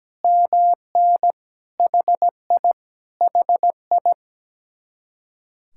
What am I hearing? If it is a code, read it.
Morse code, heard as a keyed tone, about 17 wpm, 700 Hz. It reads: MN HI HI
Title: Speed-Racing Common QSO Phrases - 17wpm